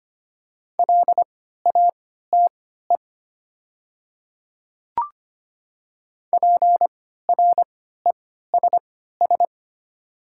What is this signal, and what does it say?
Morse code, heard as a keyed tone, about 25 wpm, spaced as for 14 wpm, 700 Hz. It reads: LATE PRESS
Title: 3x Character Spacing - Top 300 Words - 25wpm